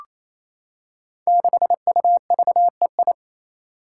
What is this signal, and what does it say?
Morse code, heard as a keyed tone, about 28 wpm, 700 Hz. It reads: 6UVEI
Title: Five Challenging Letter-Number Combinations 28wpm